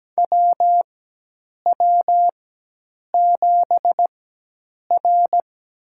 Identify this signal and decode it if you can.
Morse code, heard as a keyed tone, about 17 wpm, spaced as for 8 wpm, 700 Hz. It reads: WW7R
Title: USA Call Signs - 4 Characters - 4x Character Spacing - 17wpm